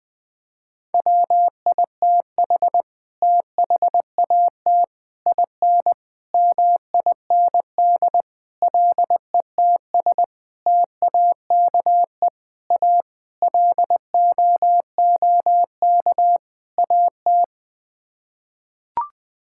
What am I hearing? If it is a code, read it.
Morse code, heard as a keyed tone, about 20 wpm, 700 Hz. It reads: WITH THAT IN MIND LETS TAKE A LOOK AT
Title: Sets of 9 Words - 20wpm